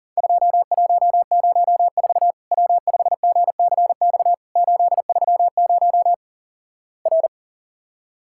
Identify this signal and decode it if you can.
Morse code, heard as a keyed tone, about 37 wpm, 700 Hz. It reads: 2104 W5GCX 830 R
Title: Worked All Europe DX Contest - QTC Exchange - 40wpm